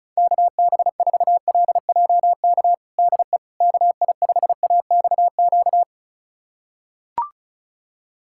Morse code, heard as a keyed tone, about 36 wpm, 700 Hz. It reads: KB4LJK DE KI5AXQ
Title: USA Callsign DE Callsign - 6 Characters - 35wpm